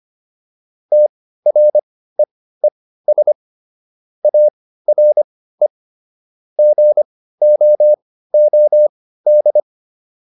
Morse code, heard as a keyed tone, about 25 wpm, spaced as for 15 wpm, 600 Hz. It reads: TREES ARE GOOD